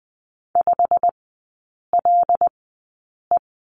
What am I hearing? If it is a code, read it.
Morse code, heard as a keyed tone, about 20 wpm, spaced as for 8 wpm, 700 Hz. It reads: 5LE
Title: 100 Lesson 17e - Letter u - Three Characters 20wpm